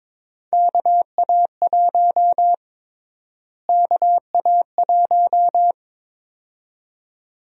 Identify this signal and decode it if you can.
Morse code, heard as a keyed tone, about 22 wpm, 700 Hz. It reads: KA1 KA1
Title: USA Call Sign Prefixes - Repeated 2x - 22wpm